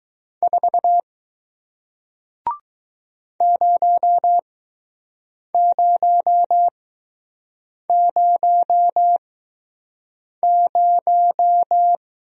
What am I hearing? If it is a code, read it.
Morse code, heard as a keyed tone, about 20 wpm, spaced as for 6 wpm, 700 Hz. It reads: 4 0000